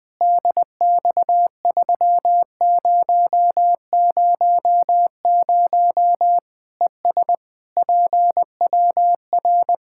Morse code, heard as a keyed tone, about 20 wpm, 700 Hz. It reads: DX3000 ES PWR
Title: Practice QSO - Exchange 2 - Info - 20wpm